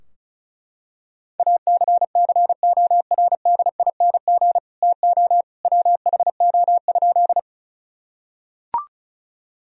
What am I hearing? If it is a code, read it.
Morse code, heard as a keyed tone, about 34 wpm, 700 Hz. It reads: ACCORDING TO WHO?
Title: Sentences from Top 1000 Words - Review Part 4 - 35wpm